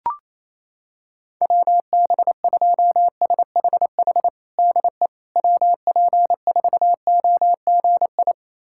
Morse code, heard as a keyed tone, about 28 wpm, 700 Hz. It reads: WB2SHH DE WP4OGI
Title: USA Callsign DE Callsign - 6 Characters - 28wpm